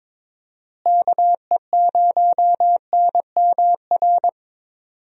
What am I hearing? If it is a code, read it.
Morse code, heard as a keyed tone, about 22 wpm, 700 Hz. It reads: KE0NMR
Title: USA Call Signs - 6 Characters - Repeated 2x - 22wpm